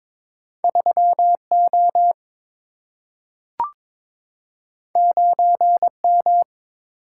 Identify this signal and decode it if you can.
Morse code, heard as a keyed tone, about 22 wpm, 700 Hz. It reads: 3 O 9 M
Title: Single-Letter Single-Number 22wpm